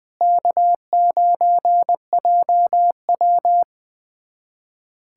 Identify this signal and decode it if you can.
Morse code, heard as a keyed tone, about 20 wpm, 700 Hz. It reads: K9JW
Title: USA Callsign DE Callsign - 4 Characters - 20wpm